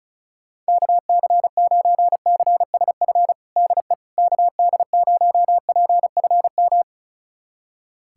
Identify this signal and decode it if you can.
Morse code, heard as a keyed tone, about 34 wpm, 700 Hz. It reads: KC9CSF DE KD0PFM
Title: USA Callsign DE Callsign - 6 Characters - 35wpm